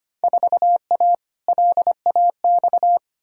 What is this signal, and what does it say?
Morse code, heard as a keyed tone, about 25 wpm, 700 Hz. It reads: 4A LAX
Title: Field Day Exchanges - 25wpm